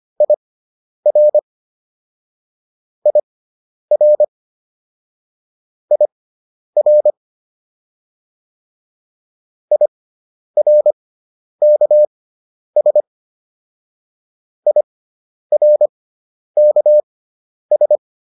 Morse code, heard as a keyed tone, about 25 wpm, spaced as for 10 wpm, 600 Hz. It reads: IR IR IR IRKS IRKS